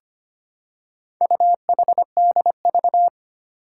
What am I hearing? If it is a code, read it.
Morse code, heard as a keyed tone, about 25 wpm, 700 Hz. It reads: UHDV
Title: Four Challenging Letter-Number Combinations 25wpm